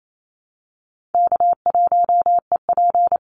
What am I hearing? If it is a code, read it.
Morse code, heard as a keyed tone, about 28 wpm, 700 Hz. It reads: K1EP
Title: Easy Call Signs Repeated 3x - 28wpm